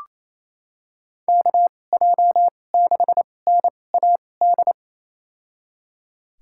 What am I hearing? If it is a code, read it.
Morse code, heard as a keyed tone, about 28 wpm, spaced as for 20 wpm, 700 Hz. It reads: KJ6NAD
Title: USA Call Signs - 6 Characters - 2x Character Spacing - 28wpm